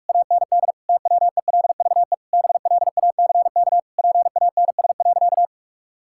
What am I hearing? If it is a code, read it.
Morse code, heard as a keyed tone, about 48 wpm, 700 Hz. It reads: AND TWELVE BLACK PANS.
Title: Pangram Challenge v2 - 45wpm